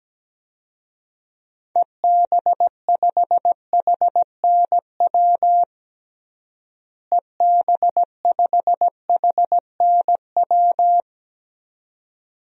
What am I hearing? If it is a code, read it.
Morse code, heard as a keyed tone, about 17 wpm, 700 Hz. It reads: EB5HNW EB5HNW
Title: DXCC-100 Call Signs - 6 Characters - Repeated 2x - 17wpm